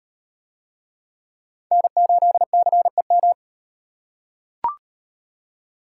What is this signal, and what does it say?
Morse code, heard as a keyed tone, about 38 wpm, 700 Hz. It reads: N8CEM